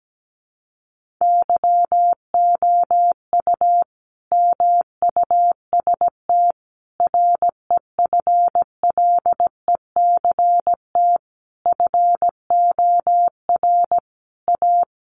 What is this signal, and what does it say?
Morse code, heard as a keyed tone, about 17 wpm, 700 Hz. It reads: YOU MUST REFLECT FOR A
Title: Sentences from Top 1500 Words 17wpm